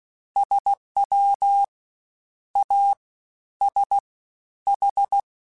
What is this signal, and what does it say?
Morse code, heard as a keyed tone, about 16 wpm, 800 Hz. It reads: SW A S H
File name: Misc / Misc-mp3s / Jumps @15